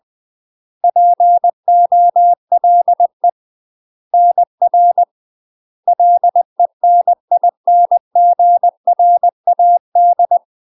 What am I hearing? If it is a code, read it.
Morse code, heard as a keyed tone, about 20 wpm, 700 Hz. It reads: POLE NR LENINGRAD
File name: MASTER QSOs TRAINING/20WPM